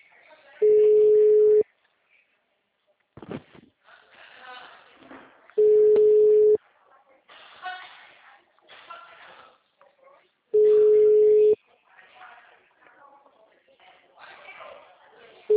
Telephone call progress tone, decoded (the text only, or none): ringback tone